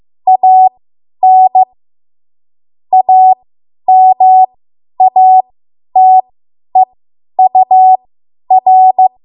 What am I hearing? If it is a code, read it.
Morse code, heard as a keyed tone, about 15 wpm, spaced as for 10 wpm, 750 Hz. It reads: AN AMATEUR